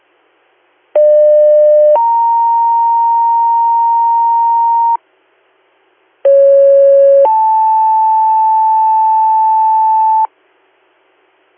Two-tone page, heard as a one-second tone, then a three-second tone, about 600 then 930 Hz, about 550 then 880 Hz. It is heard from 1.0 to 10.3 s.